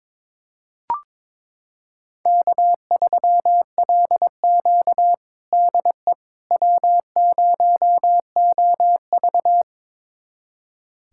Morse code, heard as a keyed tone, about 22 wpm, 700 Hz. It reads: K3LQ DE W0OV
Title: USA Callsign DE Callsign - 4 Characters - 22wpm